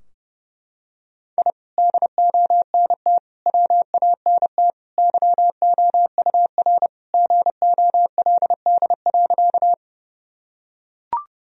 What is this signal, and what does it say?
Morse code, heard as a keyed tone, about 32 wpm, 700 Hz. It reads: I DONT WANT YOUR GOLD.
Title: Sentences from Top 1000 Words - Review Part 3 - 30wpm